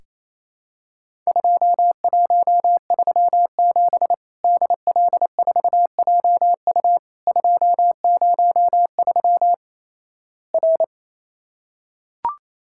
Morse code, heard as a keyed tone, about 28 wpm, 700 Hz. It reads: 2137 DL4JU 203 R
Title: Worked All Europe DX Contest - QTC Exchange - 28wpm